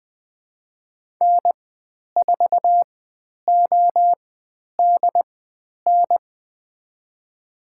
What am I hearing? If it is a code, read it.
Morse code, heard as a keyed tone, about 20 wpm, spaced as for 10 wpm, 700 Hz. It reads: N4ODN
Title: 109 Lesson 18g - Letter m - Call Signs 20wpm